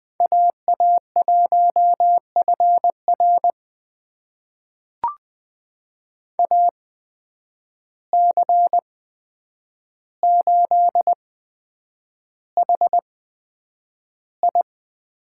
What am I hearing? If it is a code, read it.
Morse code, heard as a keyed tone, about 20 wpm, spaced as for 9 wpm, 700 Hz. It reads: AA1FR AC8HI
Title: USA Call Signs - 5 Characters - 8x Character Spacing - 20wpm